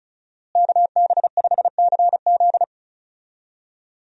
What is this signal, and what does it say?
Morse code, heard as a keyed tone, about 34 wpm, 700 Hz. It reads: KB5CZ
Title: USA Call Signs - 5 Characters - Repeated 2x - 35wpm